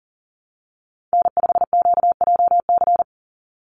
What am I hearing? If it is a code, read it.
Morse code, heard as a keyed tone, about 40 wpm, 700 Hz. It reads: N5QJC
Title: USA Call Signs 40wpm